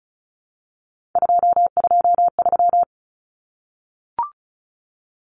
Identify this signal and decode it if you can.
Morse code, heard as a keyed tone, about 34 wpm, 700 Hz. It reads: 223
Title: Common Cut Numbers - 3-Digit 35wpm